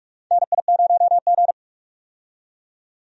Morse code, heard as a keyed tone, about 46 wpm, 700 Hz. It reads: NI0G
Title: USA Call Signs - 4 Characters - 45wpm